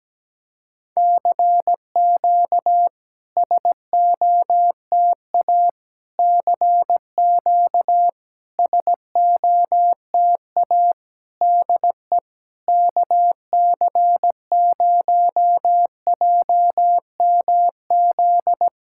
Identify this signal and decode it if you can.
Morse code, heard as a keyed tone, about 17 wpm, 700 Hz. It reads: CQ SOTA CQ SOTA DE KC0JMZ
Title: SOTA QSO Practice 17wpm